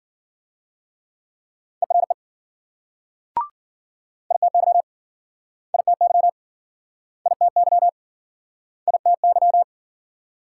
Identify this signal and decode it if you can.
Morse code, heard as a keyed tone, about 52 wpm, 700 Hz. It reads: ERE ITY ITY ITY ITY